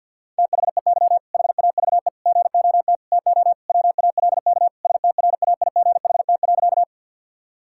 Morse code, heard as a keyed tone, about 52 wpm, 700 Hz. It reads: THEY HAVE GOT TO WALK STRAIGHT.
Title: Sentences from Top 1000 Words - Review Part 6 - 50wpm